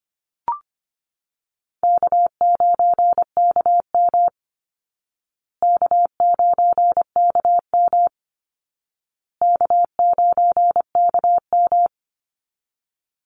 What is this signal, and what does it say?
Morse code, heard as a keyed tone, about 25 wpm, 700 Hz. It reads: K9KM K9KM K9KM
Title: Easy Call Signs Repeated 3x - 25wpm